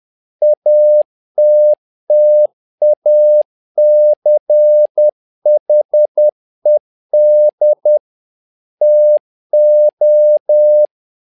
Morse code, heard as a keyed tone, about 10 wpm, 600 Hz. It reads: ATTACHED TO
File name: learning center / tower trouble / tower trouble 10wpm